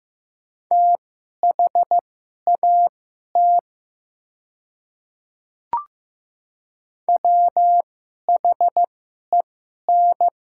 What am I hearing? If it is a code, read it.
Morse code, heard as a keyed tone, about 15 wpm, spaced as for 11 wpm, 700 Hz. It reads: THAT WHEN
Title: 2x Character Spacing - Top 100 Words - 15wpm